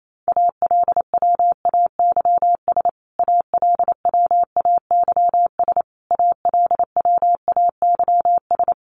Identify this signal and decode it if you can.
Morse code, heard as a keyed tone, about 28 wpm, 700 Hz. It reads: ALWAYS ALWAYS ALWAYS
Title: Top 300 Words Repeated 28wpm